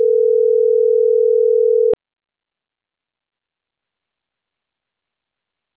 Ringback tone (North America), 440 + 480 Hz.